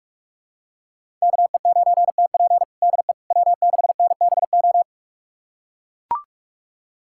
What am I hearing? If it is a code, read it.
Morse code, heard as a keyed tone, about 43 wpm, 700 Hz. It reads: KE9TP DE W6NBO
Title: USA Callsign DE Callsign - 5 Characters - 45wpm